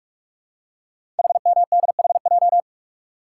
Morse code, heard as a keyed tone, about 44 wpm, 690 Hz. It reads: HMDHJ
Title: Code Groups - 5 Letters - 45wpm